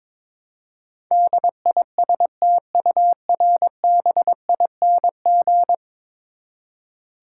Morse code, heard as a keyed tone, about 22 wpm, 700 Hz. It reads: DISTURBING
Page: morse-code-ninja-practice-22wpm